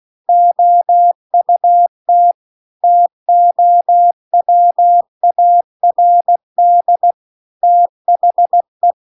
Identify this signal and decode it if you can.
Morse code, heard as a keyed tone, about 16 wpm, 700 Hz. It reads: OUT TOWARD THE